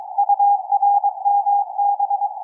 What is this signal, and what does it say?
Morse code, heard as a keyed tone, about 22 wpm, 810 Hz. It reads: URMB
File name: XW2E-CAS3E-VERT-01 18h02 utc 145mhz910 cw